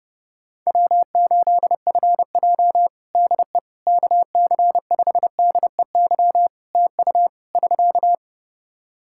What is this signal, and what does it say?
Morse code, heard as a keyed tone, about 30 wpm, 700 Hz. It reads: W8FJ DE KC5DEY TU <SK>
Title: Practice QSO - Exchange 4 - Ending - 30wpm